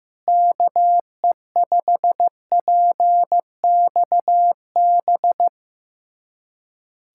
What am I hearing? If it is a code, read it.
Morse code, heard as a keyed tone, about 16 wpm, 700 Hz. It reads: KE5PXB
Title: USA Callsign DE Callsign 15wpm